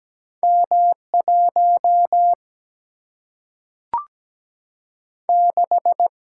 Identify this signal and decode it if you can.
Morse code, heard as a keyed tone, about 17 wpm, 700 Hz. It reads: M 1 6
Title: Single-Letter Single-Number 17wpm